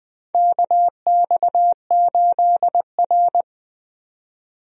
Morse code, heard as a keyed tone, about 20 wpm, 700 Hz. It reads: KX8R
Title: USA Call Signs - 4 Characters - 20wpm